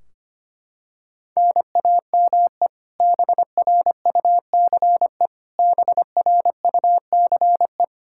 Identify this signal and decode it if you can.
Morse code, heard as a keyed tone, about 25 wpm, 700 Hz. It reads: NAME BRUCE BRUCE
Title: Practice QSO - Exchange 1 - Beginning - 25wpm